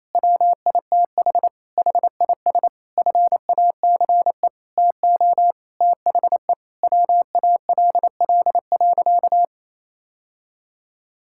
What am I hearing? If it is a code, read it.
Morse code, heard as a keyed tone, about 28 wpm, 700 Hz. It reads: WITH HIS FACE TO THE WALL.
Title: Sentences from Top 1000 Words - Review Part 6 - 28wpm